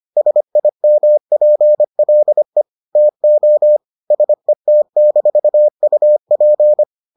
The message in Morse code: SIMPLE TO SET-UP

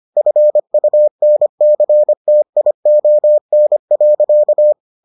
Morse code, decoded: FUNCTION.

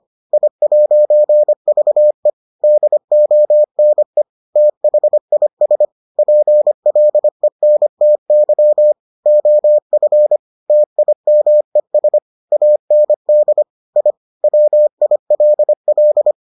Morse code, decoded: I'VE DONE THIS PLENTY OF TIMES AND I WILL

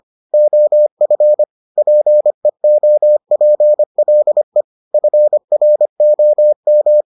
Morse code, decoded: OF PEOPLE FROM

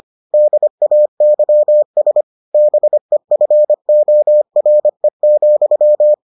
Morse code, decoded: DAYS BEFORE,

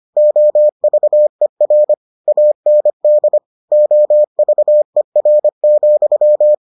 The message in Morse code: OVER AND OVER,

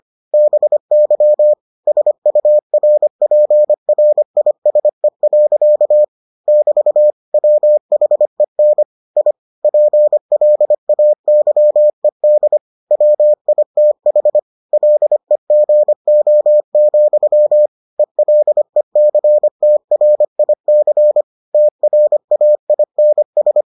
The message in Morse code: BY SURPRISE. <BT> WHEN I PLAYED WITH LEGO, ELECTRIC TRAINS

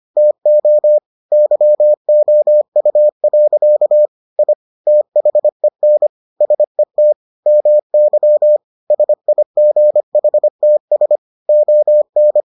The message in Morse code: TO YOU. I THEN SET MY SIGHTS ON

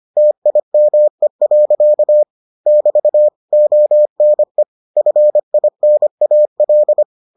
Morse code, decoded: TIME. <BT> ONE FINAL

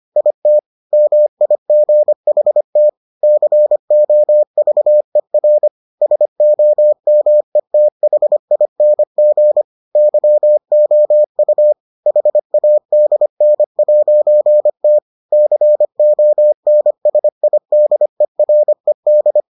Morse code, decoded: IT MIGHT COVER SOMETHING YOU HADN'T CONSIDERED

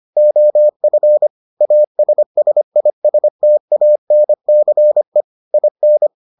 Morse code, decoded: OF ASSISTANCE IN

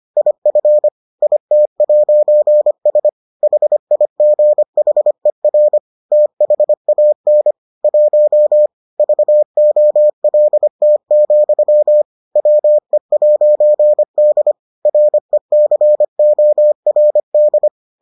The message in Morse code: IF IT'S HIGHER THAN 1 VOLT, WE'D RECORD

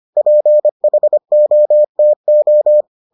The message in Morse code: PHOTO

